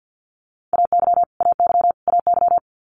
Morse code, heard as a keyed tone, about 50 wpm, 700 Hz. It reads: AY AY AY
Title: Multi-letter Phonemes Repeated 3x 50wpm